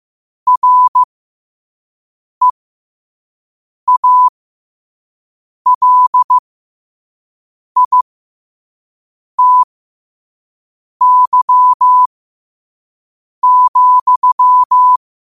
Morse code, heard as a keyed tone, about 15 wpm, spaced as for 5 wpm, 1000 Hz. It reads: REALITY,
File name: qotdfiles / 5wpm